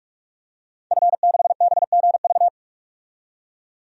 Morse code, heard as a keyed tone, about 41 wpm, 700 Hz. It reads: F6BGV
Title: DXCC-100 Call Signs - 5 Characters - 45wpm